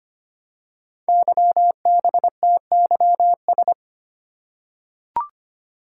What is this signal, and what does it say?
Morse code, heard as a keyed tone, about 25 wpm, 700 Hz. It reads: YBTYS
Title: Code Groups - 5 Letters - 25wpm